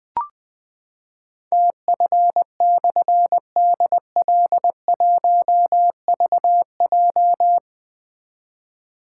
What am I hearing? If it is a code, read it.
Morse code, heard as a keyed tone, about 20 wpm, 700 Hz. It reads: TF/DL1VJ
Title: Difficult Call Signs 20wpm